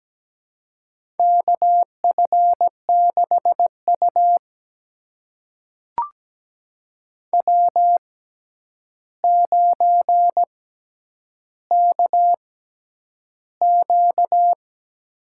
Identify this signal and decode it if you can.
Morse code, heard as a keyed tone, about 17 wpm, spaced as for 9 wpm, 700 Hz. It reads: KF6U W9KQ
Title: USA Call Signs - 4 Characters - 6x Character Spacing - 17wpm